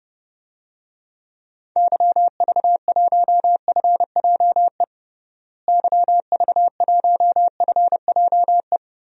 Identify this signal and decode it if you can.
Morse code, heard as a keyed tone, about 30 wpm, 700 Hz. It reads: YV1FJE YV1FJE
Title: DXCC-100 Call Signs - 6 Characters - Repeated 3x - 30wpm